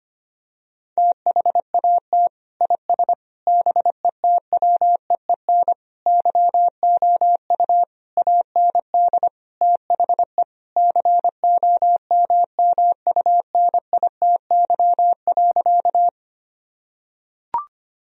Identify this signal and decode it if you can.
Morse code, heard as a keyed tone, about 25 wpm, 700 Hz. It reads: THAT IS BETWEEN YOU AND THE COMMUNITY.
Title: Sentences from Top 1000 Words - Review Part 4 - 25wpm